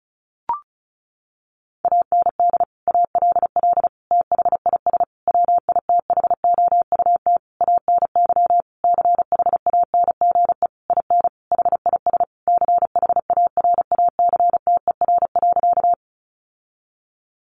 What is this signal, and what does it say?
Morse code, heard as a keyed tone, about 34 wpm, 700 Hz. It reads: AND ALL THIS WITHOUT ANY CHANGE IN HIS CHARACTER.
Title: Sentences from Top 1200 Words 35wpm